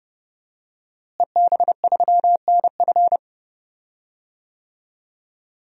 Morse code, heard as a keyed tone, about 30 wpm, 700 Hz. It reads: EB3NF
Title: DXCC-100 Call Signs - 5 Characters - Repeated 3x - 30wpm